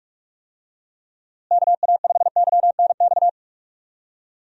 Morse code, heard as a keyed tone, about 41 wpm, 700 Hz. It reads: KA5YNX
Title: USA Call Signs - 6 Characters - Repeated 2x - 45wpm